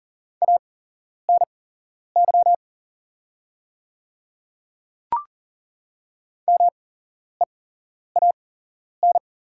Morse code, heard as a keyed tone, about 41 wpm, spaced as for 11 wpm, 700 Hz. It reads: ANY MEAN
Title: 8x Character Spacing - Top 300 Words - 40wpm